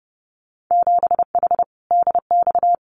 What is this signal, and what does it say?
Morse code, heard as a keyed tone, about 30 wpm, 700 Hz. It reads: 7H DX
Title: Winter Field Day Exchanges Encore 30wpm